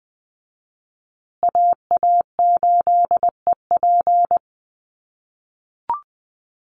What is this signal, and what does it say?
Morse code, heard as a keyed tone, about 20 wpm, 700 Hz. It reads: AA8EP
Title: USA Call Signs Repeated 3x 20wpm